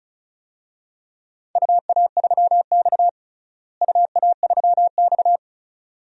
Morse code, heard as a keyed tone, about 34 wpm, 700 Hz. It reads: UA3X UA3X
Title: DXCC-100 Call Signs - 4 Characters - Repeated 2x - 35wpm